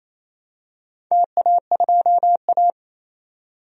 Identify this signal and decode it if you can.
Morse code, heard as a keyed tone, about 28 wpm, 700 Hz. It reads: TA2A
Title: DXCC-100 Call Signs - 4 Characters - 28wpm